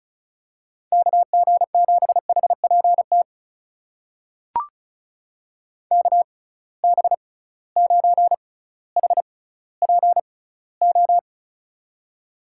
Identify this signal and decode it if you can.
Morse code, heard as a keyed tone, about 36 wpm, spaced as for 18 wpm, 700 Hz. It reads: KG7HPT KB9HPO
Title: USA Call Signs - 6x Character Spacing - 35wpm